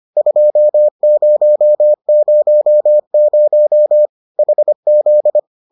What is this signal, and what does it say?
Morse code, heard as a keyed tone, about 25 wpm, 600 Hz. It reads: 2000 HZ